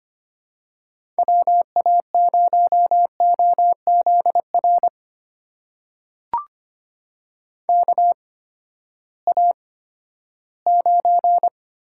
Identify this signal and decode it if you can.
Morse code, heard as a keyed tone, about 25 wpm, spaced as for 14 wpm, 700 Hz. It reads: WA0OZR KA9
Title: USA Call Signs - 6 Characters - 8x Character Spacing - 25wpm